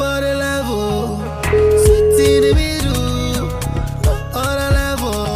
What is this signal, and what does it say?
Ringback tone (Europe, 425 Hz), with noise.